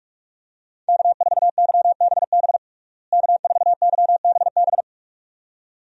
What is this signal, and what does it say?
Morse code, heard as a keyed tone, about 43 wpm, 700 Hz. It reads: K4YBB K4YBB
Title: USA Call Signs - 5 Characters - Repeated 2x - 45wpm